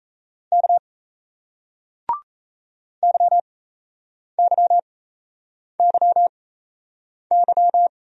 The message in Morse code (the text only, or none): K YYYY